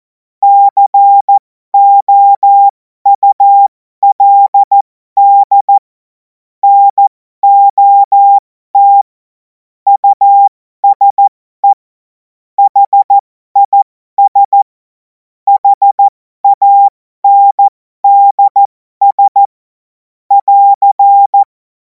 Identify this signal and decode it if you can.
Morse code, heard as a keyed tone, about 14 wpm, spaced as for 12 wpm, 800 Hz. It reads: COULD NOT USE HIS HANDS <AR>
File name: Morsekurs / Witze